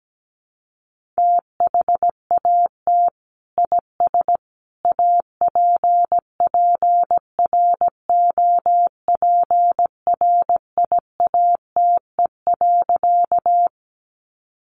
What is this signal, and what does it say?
Morse code, heard as a keyed tone, about 17 wpm, 700 Hz. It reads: THAT IS APPROPRIATE.
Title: Sentences from Top 900 Words 17wpm